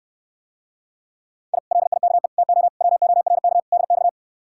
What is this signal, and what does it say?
Morse code, heard as a keyed tone, about 69 wpm, 700 Hz. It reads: I LIKE TO WORK DX